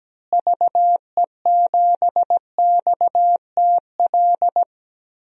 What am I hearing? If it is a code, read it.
Morse code, heard as a keyed tone, about 17 wpm, 700 Hz. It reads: VE7XTL
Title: DXCC-100 Call Signs - 6 Characters - Repeated 3x - 17wpm